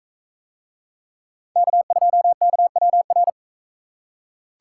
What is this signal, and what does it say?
Morse code, heard as a keyed tone, about 42 wpm, 700 Hz. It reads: K2KWR